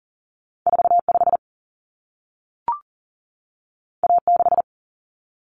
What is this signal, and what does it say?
Morse code, heard as a keyed tone, about 40 wpm, 700 Hz. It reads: 45 A6
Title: Common Cut Numbers - 2-Digit 40wpm